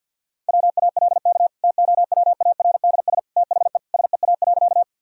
Morse code, heard as a keyed tone, about 52 wpm, 700 Hz. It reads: WALK TOWARDS THE SEA.
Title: Sentences from Top 1000 Words - Review Part 2 - 50wpm